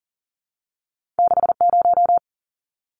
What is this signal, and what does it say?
Morse code, heard as a keyed tone, about 40 wpm, 700 Hz. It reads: 60
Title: Numbers - 2-Digit 40wpm